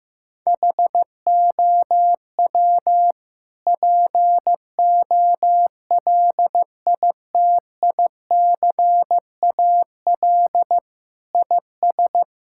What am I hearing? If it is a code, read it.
Morse code, heard as a keyed tone, about 16 wpm, 700 Hz. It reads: HOW POLITICAL IS
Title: Sentences from Top 1000 Words - Review Part 5 - 15wpm